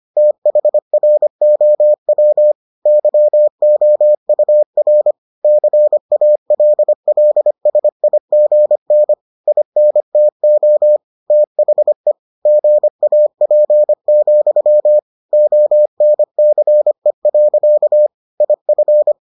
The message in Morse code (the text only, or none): THROW YOUR CALLSIGN INTO THE GAP, ONCE. IF